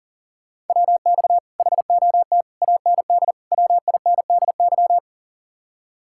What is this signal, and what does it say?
Morse code, heard as a keyed tone, about 52 wpm, 700 Hz. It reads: WX HOT AND WINDY